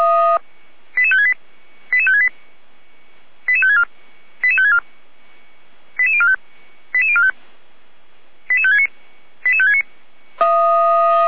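Five-tone selective call (ZVEI), heard as quick runs of five tones, tones of about 70 ms. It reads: 88568 88568 88564 88564 88045 88045 88569 88569
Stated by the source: ZVEI five-tone call 88102 received by SDR, archived by audio.f85360.feuerwehr-orlamuende.org fire brigade.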